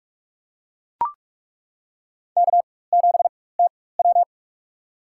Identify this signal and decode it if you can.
Morse code, heard as a keyed tone, about 42 wpm, spaced as for 20 wpm, 700 Hz. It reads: K7TW
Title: USA Call Signs - 4 Characters - 4x Character Spacing - 45wpm